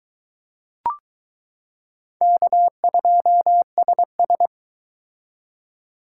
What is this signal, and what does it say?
Morse code, heard as a keyed tone, about 22 wpm, 700 Hz. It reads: K2SS